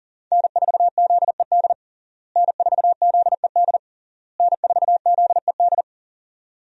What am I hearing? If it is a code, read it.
Morse code, heard as a keyed tone, about 37 wpm, 700 Hz. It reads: N4ZED N4ZED N4ZED
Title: USA Call Signs - 5 Characters - Repeated 3x - 40wpm